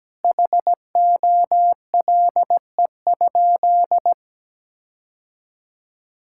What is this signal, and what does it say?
Morse code, heard as a keyed tone, about 17 wpm, 700 Hz. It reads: HOLE?
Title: Sentences from Top 1600 Words 17wpm